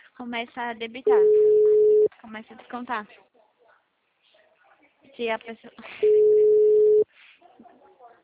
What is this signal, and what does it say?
Ringback tone (Europe, 425 Hz), with speech.